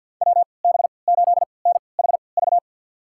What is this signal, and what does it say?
Morse code, heard as a keyed tone, about 52 wpm, spaced as for 29 wpm, 700 Hz. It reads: WB8NHV
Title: USA Call Signs - 6 Characters - 3x Character Spacing - 50wpm